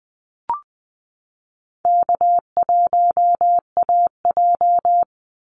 Morse code, heard as a keyed tone, about 20 wpm, 700 Hz. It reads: K1AJ
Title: Call Signs Repeated 3x - 20wpm